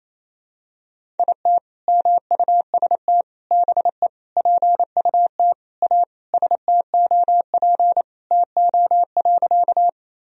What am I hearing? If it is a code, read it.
Morse code, heard as a keyed tone, about 28 wpm, 700 Hz. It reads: IT MUST BE PUT A STOP TO.
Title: Sentences from Top 1000 Words - Review Part 1 - 28wpm